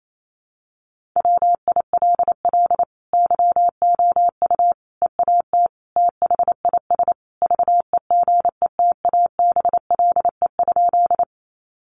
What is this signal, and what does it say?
Morse code, heard as a keyed tone, about 28 wpm, 700 Hz. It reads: WILL YOU EAT THIS VEGETABLE?
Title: Sentences from Top 1300 Words 28wpm